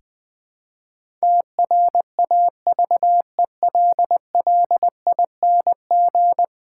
Morse code, heard as a keyed tone, about 20 wpm, 700 Hz. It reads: TRAVELLING
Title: Pangram Challenge v2 - 20wpm